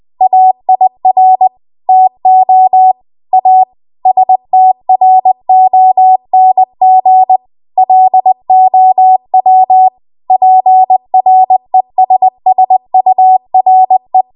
Morse code, heard as a keyed tone, about 20 wpm, 750 Hz. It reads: AIR TO A STRONG LOW PRESSURE
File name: Archive / 20 WPM / 20WPM